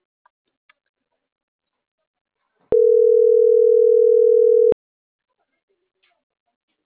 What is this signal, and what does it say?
Ringback tone (North America), 440 + 480 Hz.
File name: CENTRAL MAYAB / out atzhyry.gomez - .77711